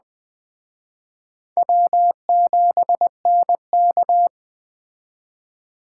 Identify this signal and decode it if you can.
Morse code, heard as a keyed tone, about 20 wpm, 700 Hz. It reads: W7NK
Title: Speed Racing - USA Call Signs 17wpm